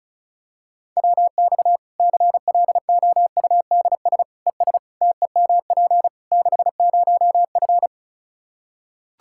Morse code, heard as a keyed tone, about 36 wpm, 700 Hz. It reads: WX CLOUDS ES TEMP 60F
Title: Practice QSO - Exchange 2 - Info - 35wpm